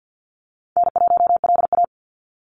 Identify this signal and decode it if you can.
Morse code, heard as a keyed tone, about 50 wpm, 700 Hz. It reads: N1LA
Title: USA Call Signs 50wpm